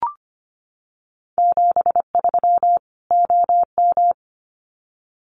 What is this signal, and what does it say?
Morse code, heard as a keyed tone, about 25 wpm, 700 Hz. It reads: 73 OM
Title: Common QSO Phrases 25wpm